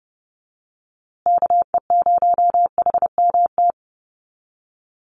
Morse code, heard as a keyed tone, about 30 wpm, 700 Hz. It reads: KE0HMT
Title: USA Call Signs Repeated 2x 30wpm